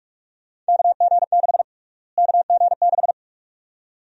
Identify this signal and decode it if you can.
Morse code, heard as a keyed tone, about 42 wpm, 700 Hz. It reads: KG6 KG6
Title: USA Call Sign Prefixes - Repeated 2x - 45wpm